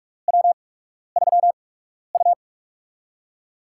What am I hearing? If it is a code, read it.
Morse code, heard as a keyed tone, about 46 wpm, spaced as for 12 wpm, 700 Hz. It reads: W3U
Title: USA Call Signs - 8x Character Spacing - 45wpm